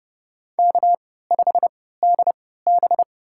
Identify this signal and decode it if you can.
Morse code, heard as a keyed tone, about 31 wpm, spaced as for 17 wpm, 700 Hz. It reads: K5DB
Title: USA Call Signs - 4 Characters - 3x Character Spacing - 30wpm